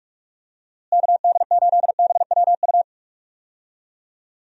Morse code, heard as a keyed tone, about 46 wpm, 700 Hz. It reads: KD8BWU
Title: USA Call Signs - 6 Characters - Repeated 2x - 45wpm